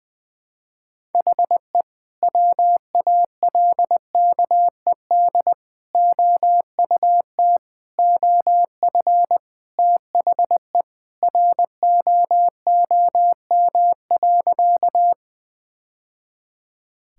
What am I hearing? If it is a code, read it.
Morse code, heard as a keyed tone, about 20 wpm, 700 Hz. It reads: HE WALKED OUT OF THE ROOM.
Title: Sentences with 2-Word Phrases - 20wpm